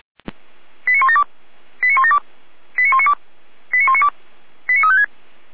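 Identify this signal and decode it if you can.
Five-tone selective call (ZVEI), heard as quick runs of five tones, tones of about 80 ms. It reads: 89172 89172 89192 89192 89356
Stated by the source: Five-tone ZVEI selective call received on SDR.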